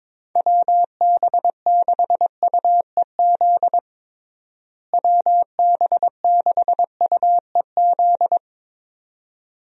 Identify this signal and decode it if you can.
Morse code, heard as a keyed tone, about 22 wpm, 700 Hz. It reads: WB6UEZ WB6UEZ
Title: USA Call Signs - 6 Characters - Repeated 2x - 22wpm